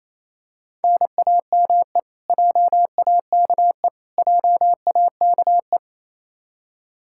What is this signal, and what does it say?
Morse code, heard as a keyed tone, about 28 wpm, 700 Hz. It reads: NAME JAKE JAKE
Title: Practice QSO - Exchange 1 - Beginning - 28wpm